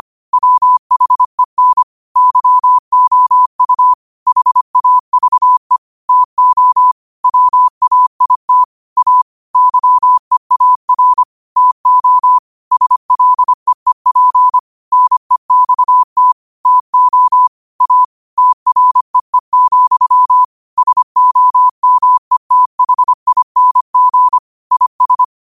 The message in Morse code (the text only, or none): WHEN YOU HAVE TO WAIT A YEAR TO SLEEP NEXT TO A TREE, SOMETHING IS